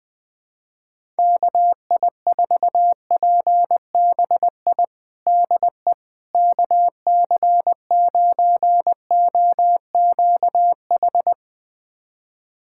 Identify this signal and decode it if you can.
Morse code, heard as a keyed tone, about 20 wpm, 700 Hz. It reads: KI4PBI DE KC9OQH
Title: USA Callsign DE Callsign 20wpm